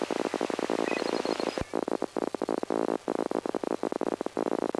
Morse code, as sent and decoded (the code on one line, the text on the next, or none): none